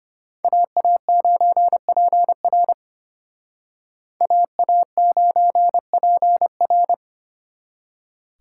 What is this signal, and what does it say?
Morse code, heard as a keyed tone, about 30 wpm, 700 Hz. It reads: AA9PR AA9PR
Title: Speed Racing - USA Call Signs 25wpm